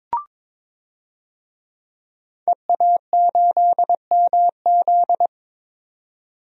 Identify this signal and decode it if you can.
Morse code, heard as a keyed tone, about 22 wpm, 700 Hz. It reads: EA8MZ
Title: DXCC-100 Call Signs - 5 Characters - 22wpm